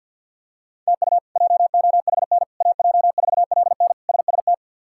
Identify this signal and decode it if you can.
Morse code, heard as a keyed tone, about 49 wpm, 700 Hz. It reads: TU JOHN AJ4LN SST
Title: K1USN SST QSO Practice - 50wpm